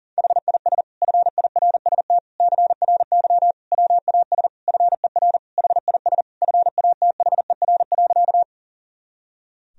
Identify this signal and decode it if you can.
Morse code, heard as a keyed tone, about 40 wpm, 700 Hz. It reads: HIS FIRST CRY WAS FER HIS FATHER.
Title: Sentences from Top 500 Words and Abbreviations 40wpm